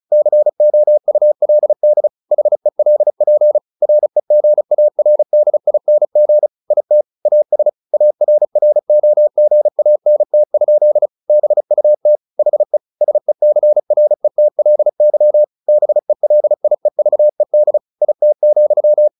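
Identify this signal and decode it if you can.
Morse code, heard as a keyed tone, about 34 wpm, 600 Hz. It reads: COULD HELP REGARDING IT AS ARROGANT? BUT HE SECRETLY BELIEVED IT,
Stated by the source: Morse practice at 35 wpm from CW med Gnister, sending Ken Follett's novel Never in English.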